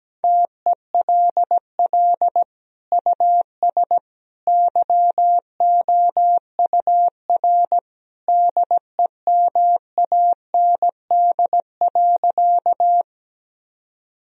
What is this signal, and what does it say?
Morse code, heard as a keyed tone, about 17 wpm, 700 Hz. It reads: TELL US YOUR DEMAND.
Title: Sentences from Top 1000 Words - Review Part 4 - 17wpm